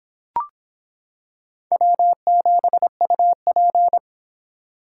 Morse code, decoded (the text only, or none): W7UP